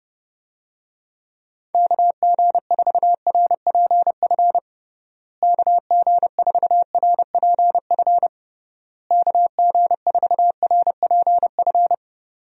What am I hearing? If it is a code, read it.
Morse code, heard as a keyed tone, about 29 wpm, 700 Hz. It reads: KG4RPF KG4RPF KG4RPF
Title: USA Call Signs - 6 Characters - Repeated 3x - 30wpm